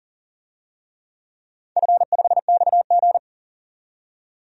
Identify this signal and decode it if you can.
Morse code, heard as a keyed tone, about 40 wpm, 700 Hz. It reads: F5XG
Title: DXCC-100 Call Signs - 4 Characters - 40wpm